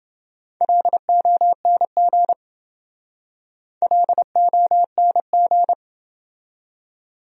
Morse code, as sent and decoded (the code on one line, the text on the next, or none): .-.. --- -. --. / .-.. --- -. --.
LONG LONG